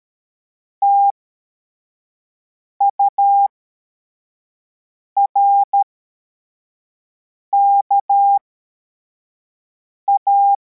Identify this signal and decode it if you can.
Morse code, heard as a keyed tone, about 13 wpm, spaced as for 4 wpm, 800 Hz. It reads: TURKA
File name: formation / cw / Text55